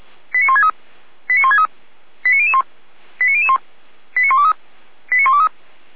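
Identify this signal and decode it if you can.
Five-tone selective call (ZVEI), heard as quick runs of five tones, tones of about 70 ms. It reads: 89163 89163 89001 89001 89123 89123
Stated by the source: ZVEI five-tone selective call received by SDR.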